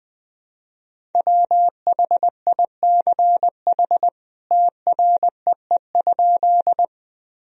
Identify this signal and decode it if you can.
Morse code, heard as a keyed tone, about 19 wpm, 700 Hz. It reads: WHICH TREE?
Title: Sentences from Top 1000 Words - Review Part 1 - 20wpm